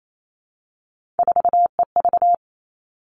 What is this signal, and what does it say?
Morse code, heard as a keyed tone, about 28 wpm, 700 Hz. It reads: 4EV
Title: Three Challenging Letter-Number Combinations 28wpm